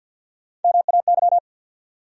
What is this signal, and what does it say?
Morse code, heard as a keyed tone, about 52 wpm, 700 Hz. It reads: MAY
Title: Top 100 Words 50wpm